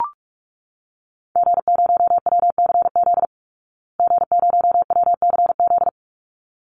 Morse code, heard as a keyed tone, about 44 wpm, 700 Hz. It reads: G0WCZ G0WCZ
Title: Call Signs Repeated 3x - 45wpm